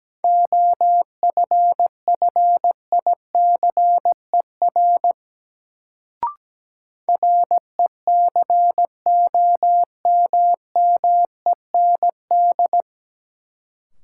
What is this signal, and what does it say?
Morse code, heard as a keyed tone, about 17 wpm, 700 Hz. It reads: OFFICER RECOMMEND
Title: Sentences from Top 1200 Words - New Word Drill - 17wpm